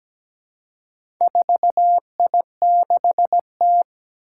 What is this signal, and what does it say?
Morse code, heard as a keyed tone, about 17 wpm, 700 Hz. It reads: 4I6T
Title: Four Challenging Letter-Number Combinations 17wpm